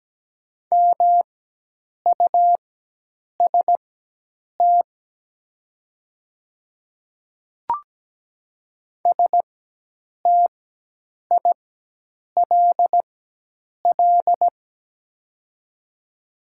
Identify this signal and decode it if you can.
Morse code, heard as a keyed tone, about 17 wpm, spaced as for 8 wpm, 700 Hz. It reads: MUST STILL
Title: 4x Character Spacing - Top 500 Words - 17wpm